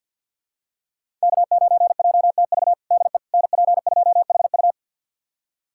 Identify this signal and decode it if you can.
Morse code, heard as a keyed tone, about 52 wpm, 690 Hz. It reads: K9JTV DE NP2HU
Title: USA Callsign DE Callsign - 5 Characters - 50wpm